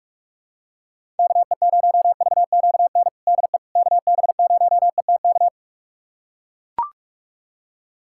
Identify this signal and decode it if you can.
Morse code, heard as a keyed tone, about 41 wpm, 700 Hz. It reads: KE0VQN DE KB0ETK
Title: USA Callsign DE Callsign 45wpm